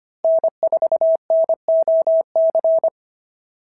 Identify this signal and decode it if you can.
Morse code, heard as a keyed tone, about 23 wpm, 650 Hz. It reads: N4NOC
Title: SOTA QSO Practice 25wpm